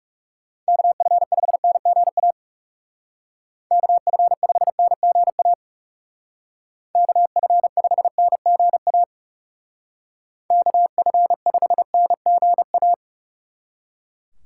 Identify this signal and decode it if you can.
Morse code, heard as a keyed tone, about 36 wpm, 700 Hz. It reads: KF5NGA KF5NGA KF5NGA KF5NGA
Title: Speed Racing - USA 2x3 Call Signs 30wpm